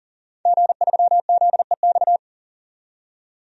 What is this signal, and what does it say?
Morse code, heard as a keyed tone, about 38 wpm, 700 Hz. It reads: G3ZEX
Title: DXCC-100 Call Signs - 5 Characters - 40wpm